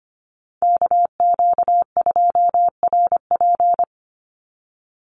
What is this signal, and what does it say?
Morse code, heard as a keyed tone, about 25 wpm, 700 Hz. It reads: KQ2RP
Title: NA SOTA Chaser Callsigns 25wpm